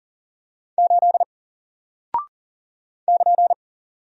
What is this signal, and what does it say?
Morse code, heard as a keyed tone, about 40 wpm, 700 Hz. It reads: 8 <KN>
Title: Single Character (Letter:Number:Punctuation:Prosign) Alt 40wpm